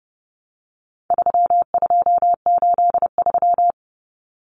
Morse code, heard as a keyed tone, about 29 wpm, 700 Hz. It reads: 3283
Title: Common Cut Numbers - 4-Digit 30wpm